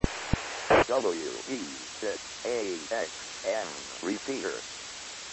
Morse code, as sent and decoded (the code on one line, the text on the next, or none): none